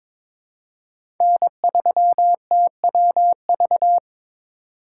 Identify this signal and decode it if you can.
Morse code, heard as a keyed tone, about 22 wpm, 700 Hz. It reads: N3TWV
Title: USA Call Signs - 5 Characters - 2x Character Spacing - 22wpm